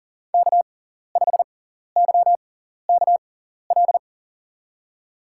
Morse code, heard as a keyed tone, about 40 wpm, spaced as for 14 wpm, 700 Hz. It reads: K5YKL
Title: USA Call Signs - 5 Characters - 6x Character Spacing - 40wpm